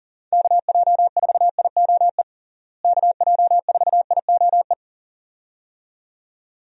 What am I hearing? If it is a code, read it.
Morse code, heard as a keyed tone, about 40 wpm, 700 Hz. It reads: KJ4IOE KJ4IOE
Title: USA Call Signs - 6 Characters - Repeated 2x - 40wpm